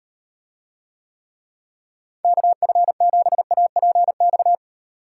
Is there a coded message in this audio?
Yes: Morse code from 2.2 to 4.5 s, heard as a keyed tone, about 36 wpm, 700 Hz. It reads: KF7APX